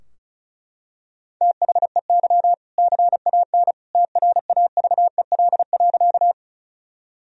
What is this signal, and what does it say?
Morse code, heard as a keyed tone, about 34 wpm, 700 Hz. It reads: THEY CAN TRAVEL.
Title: Sentences from Top 1000 Words - Review Part 2 - 35wpm